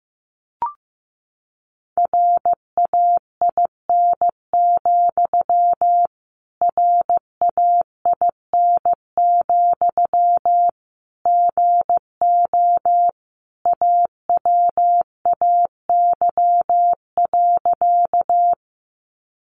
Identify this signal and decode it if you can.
Morse code, heard as a keyed tone, about 15 wpm, 700 Hz. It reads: RAIN, RAIN, GO AWAY.
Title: Sentences from Top 600 Words 15wpm